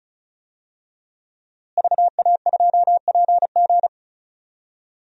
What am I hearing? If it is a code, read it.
Morse code, heard as a keyed tone, about 36 wpm, 700 Hz. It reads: VA2PG
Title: DXCC-100 Call Signs - 5 Characters - Repeated 3x - 35wpm